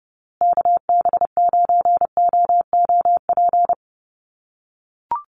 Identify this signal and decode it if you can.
Morse code, heard as a keyed tone, about 30 wpm, 700 Hz. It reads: KB9OOP
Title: USA Call Signs Repeated 3x 30wpm